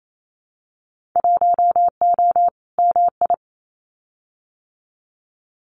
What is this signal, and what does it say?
Morse code, heard as a keyed tone, about 28 wpm, 700 Hz. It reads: 1O MI
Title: Winter Field Day Exchanges Encore 28wpm